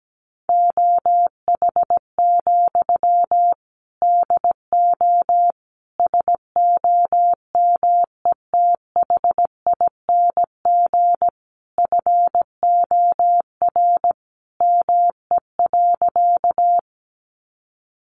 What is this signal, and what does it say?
Morse code, heard as a keyed tone, about 17 wpm, 700 Hz. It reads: OH, DO SOMETHING FOR ME.
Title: Sentences from Top 600 Words 17wpm